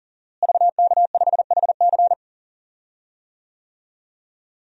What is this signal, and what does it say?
Morse code, heard as a keyed tone, about 40 wpm, 700 Hz. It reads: VK5HC
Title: DXCC-100 Call Signs - 5 Characters - Repeated 3x - 40wpm